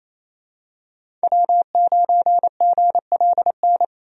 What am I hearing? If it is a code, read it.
Morse code, heard as a keyed tone, about 28 wpm, 700 Hz. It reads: W9GLN
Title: USA Call Signs - 5 Characters - 28wpm